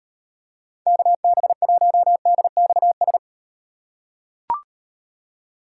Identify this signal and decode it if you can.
Morse code, heard as a keyed tone, about 38 wpm, 700 Hz. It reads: KB1DXS